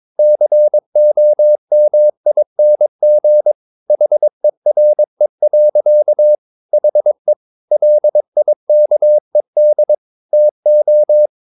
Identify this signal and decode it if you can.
Morse code, heard as a keyed tone, about 22 wpm, 600 Hz. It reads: COMING HERE. HE LIKED TO
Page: Romanen Never af Ken Follet på engelsk.